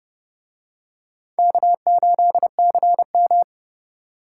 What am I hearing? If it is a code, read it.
Morse code, heard as a keyed tone, about 29 wpm, 700 Hz. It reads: K8CM